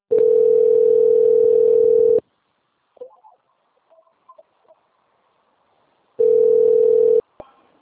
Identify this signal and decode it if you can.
Ringback tone (North America), 440 + 480 Hz.
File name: out- - - - - .4736